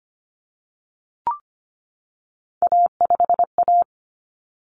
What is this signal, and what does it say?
Morse code, heard as a keyed tone, about 25 wpm, 700 Hz. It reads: A5A
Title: Common Cut Numbers - 3-Digit 25wpm